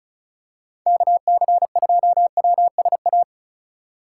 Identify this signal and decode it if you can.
Morse code, heard as a keyed tone, about 36 wpm, 700 Hz. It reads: KC2WSA
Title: USA Call Signs - 4x Character Spacing - 35wpm